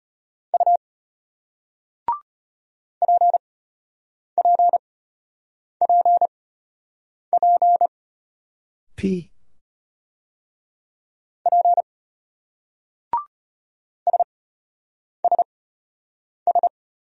Morse code, heard as a keyed tone, about 34 wpm, spaced as for 8 wpm, 700 Hz. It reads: U PPPP P SSS